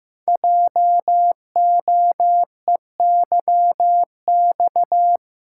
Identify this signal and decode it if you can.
Morse code, heard as a keyed tone, about 15 wpm, 700 Hz. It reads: JOEYX
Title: Code Groups - 5 Letter:Numbers - 15wpm